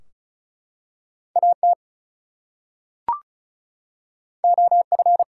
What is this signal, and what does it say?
Morse code, heard as a keyed tone, about 33 wpm, 700 Hz. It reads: AT OF
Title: Two Letter Words - 35wpm